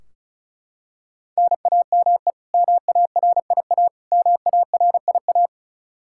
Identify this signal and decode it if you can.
Morse code, heard as a keyed tone, about 34 wpm, 700 Hz. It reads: NAME MARIA MARIA
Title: Practice QSO - Exchange 1 - Beginning - 35wpm